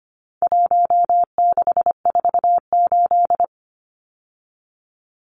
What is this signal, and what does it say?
Morse code, heard as a keyed tone, about 25 wpm, 700 Hz. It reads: 1648
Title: Numbers - 4-Digit 25wpm